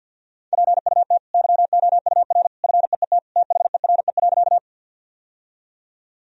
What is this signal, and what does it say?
Morse code, heard as a keyed tone, about 50 wpm, 700 Hz. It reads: PUT YOUR FEET THERE.
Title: Sentences from Top 1000 Words - Review Part 1 - 50wpm